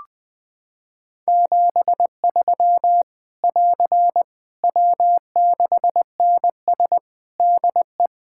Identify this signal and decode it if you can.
Morse code, heard as a keyed tone, about 21 wpm, 700 Hz. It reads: 73 <AR> W6NS DE
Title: Practice QSO - Exchange 4 - Ending - 20wpm